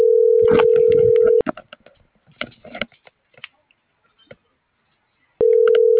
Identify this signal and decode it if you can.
Ringback tone (North America), 440 + 480 Hz.